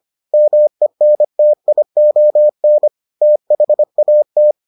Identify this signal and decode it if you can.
Morse code, heard as a keyed tone, about 25 wpm, 600 Hz. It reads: MENTION THAT